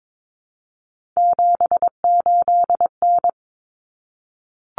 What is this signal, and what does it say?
Morse code, heard as a keyed tone, about 22 wpm, 700 Hz. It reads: 78N
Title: Common Cut Numbers - 3-Digit 22wpm